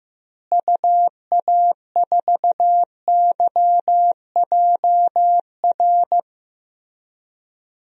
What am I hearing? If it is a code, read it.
Morse code, heard as a keyed tone, about 15 wpm, 700 Hz. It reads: UA4YJR
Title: DXCC-100 Call Signs - 6 Characters - Repeated 3x - 15wpm